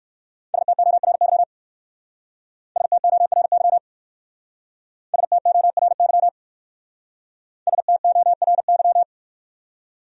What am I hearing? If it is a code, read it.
Morse code, heard as a keyed tone, about 57 wpm, 700 Hz. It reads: STORY STORY STORY STORY